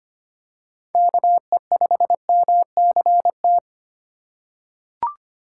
Morse code, heard as a keyed tone, about 25 wpm, 700 Hz. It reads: KE5MCT
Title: USA Call Signs - 6 Characters - 8x Character Spacing - 25wpm